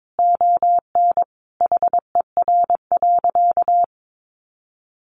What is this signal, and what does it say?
Morse code, heard as a keyed tone, about 22 wpm, 700 Hz. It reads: ON HER.
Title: Sentences from Top 1500 Words 22wpm